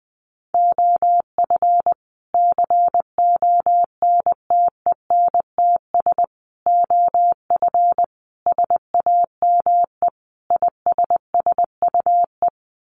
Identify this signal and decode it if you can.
Morse code, heard as a keyed tone, about 20 wpm, 700 Hz. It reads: OF CONTENTS OF SAME ISSUE
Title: Sets of 5 Words Encore2 20wpm